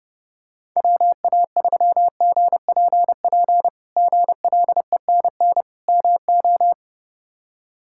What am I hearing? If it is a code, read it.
Morse code, heard as a keyed tone, about 32 wpm, 700 Hz. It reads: WA3GPP GLENN MO
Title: K1USN SST QSO Practice - 30wpm